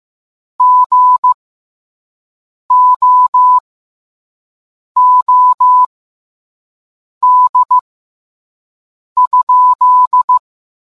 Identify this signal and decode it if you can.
Morse code, heard as a keyed tone, about 15 wpm, spaced as for 5 wpm, 1000 Hz. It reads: GOOD?